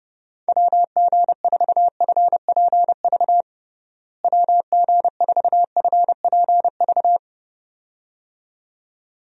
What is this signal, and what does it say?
Morse code, heard as a keyed tone, about 30 wpm, 700 Hz. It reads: WG4FPV WG4FPV
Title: USA Call Signs - 6 Characters - Repeated 2x - 30wpm